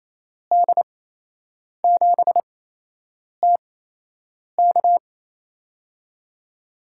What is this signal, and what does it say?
Morse code, heard as a keyed tone, about 28 wpm, spaced as for 8 wpm, 700 Hz. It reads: D7TK
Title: USA Call Signs - 5 Characters - 8x Character Spacing - 28wpm